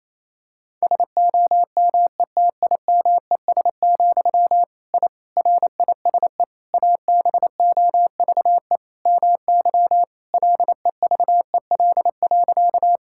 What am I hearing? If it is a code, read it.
Morse code, heard as a keyed tone, about 28 wpm, 700 Hz. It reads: SOMETIMES, I RISE ABOVE MY LEVEL.
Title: Sentences from Top 1000 Words - Review Part 6 - 28wpm